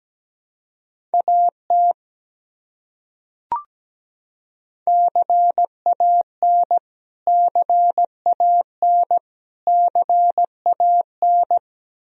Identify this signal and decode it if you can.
Morse code, heard as a keyed tone, about 17 wpm, 700 Hz. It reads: AT CAN CAN CAN
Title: Top 100 Words Repeated 17wpm